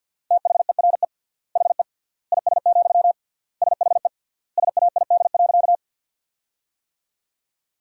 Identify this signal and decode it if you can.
Morse code, heard as a keyed tone, about 50 wpm, 700 Hz. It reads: THERE HE IS, SHE SAID.
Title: 3x Word-Spacing - Sentences from Top 100 Words - 50wpm